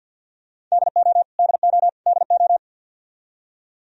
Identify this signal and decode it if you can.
Morse code, heard as a keyed tone, about 46 wpm, 700 Hz. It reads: DO DO DO
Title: Top 100 Words Repeated 50wpm